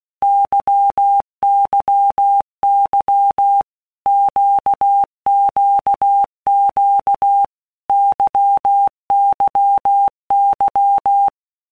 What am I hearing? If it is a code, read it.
Morse code, heard as a keyed tone, about 16 wpm, 800 Hz. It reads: YYY QQQ YYY
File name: Basic Letter Groups By Blocks / Groups@15 / LFCYQ@15